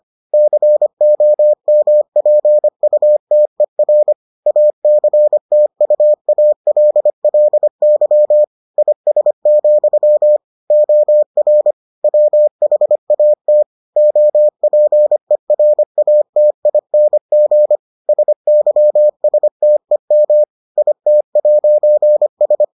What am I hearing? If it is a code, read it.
Morse code, heard as a keyed tone, about 25 wpm, 600 Hz. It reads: COMPUTER ACTUALLY IS, OR WHAT OPERATING SYSTEM IT'S